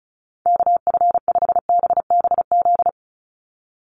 Morse code, heard as a keyed tone, about 34 wpm, 700 Hz. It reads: KF5BBZ
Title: USA Call Signs 35wpm